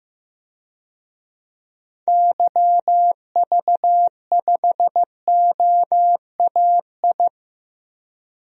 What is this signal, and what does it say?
Morse code, heard as a keyed tone, about 16 wpm, 700 Hz. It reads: YV5OAI
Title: DXCC-100 Call Signs - 6 Characters - Repeated 3x - 15wpm